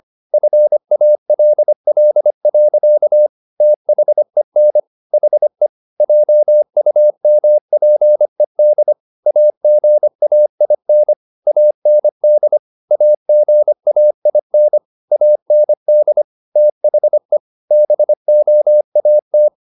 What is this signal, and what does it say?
Morse code, heard as a keyed tone, about 25 wpm, 600 Hz. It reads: FALL. THEN HE JUMPED AGAIN AND AGAIN AND THE BOAT